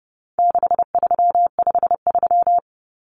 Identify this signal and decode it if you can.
Morse code, heard as a keyed tone, about 28 wpm, 700 Hz. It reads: 6353
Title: Numbers - 4-Digit 30wpm